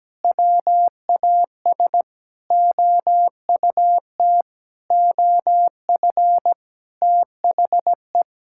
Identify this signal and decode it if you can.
Morse code, heard as a keyed tone, about 17 wpm, 700 Hz. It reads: WAS OUT OF THE
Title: Sentences from Top 1000 Words - Review Part 5 - 17wpm